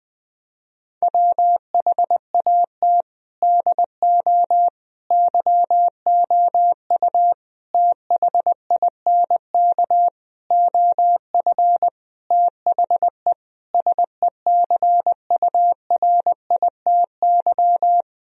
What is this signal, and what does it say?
Morse code, heard as a keyed tone, about 20 wpm, 700 Hz. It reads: WHAT DO YOU THINK OF THE SECURITY
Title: Sentences from Top 1000 Words - Review Part 4 - 20wpm